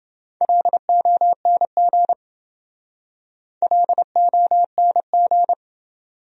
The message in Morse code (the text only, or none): LONG LONG